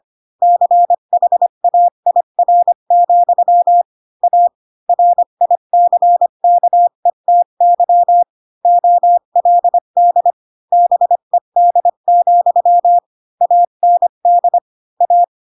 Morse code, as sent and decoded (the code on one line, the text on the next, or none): -.-. .... .- .. .-. --..-- / .- / .-. .. -.-. -.- . - -.-- / --- .-.. -.. / -... -.. --..-- / .- -. -.. / .-
CHAIR, A RICKETY OLD BED, AND A